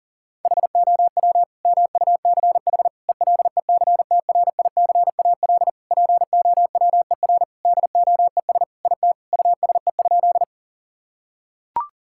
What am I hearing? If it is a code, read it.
Morse code, heard as a keyed tone, about 40 wpm, 700 Hz. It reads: HOW MUCH ELECTRICAL POWER DOES IT USE?
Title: Sentences from Top 1000 Words - Review Part 6 - 40wpm